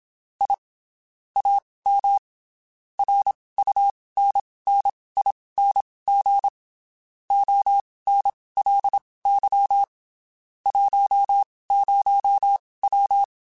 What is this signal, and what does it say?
Morse code, heard as a keyed tone, about 27 wpm, spaced as for 19 wpm, 800 Hz. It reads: I AM RUNNING ONLY 10W
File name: QSOs Spaced